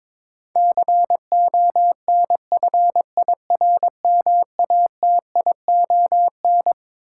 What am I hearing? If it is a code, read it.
Morse code, heard as a keyed tone, about 22 wpm, 700 Hz. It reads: CONFIRMATION
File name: Twelve Letter Words - 22wpm